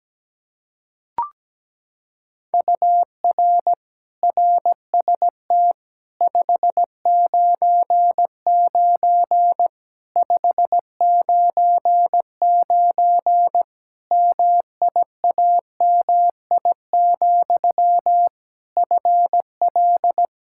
Morse code, heard as a keyed tone, about 17 wpm, 700 Hz. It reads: UR RST 599 599 MIAMI, FL
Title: Practice QSO - Exchange 1 - Beginning - 17wpm